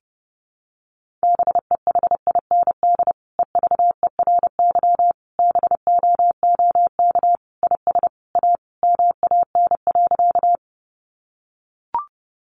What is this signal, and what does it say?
Morse code, heard as a keyed tone, about 30 wpm, 700 Hz. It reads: BEHIND EVERY BOOK IS A MAN.
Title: Sentences from Top 500 Words 30wpm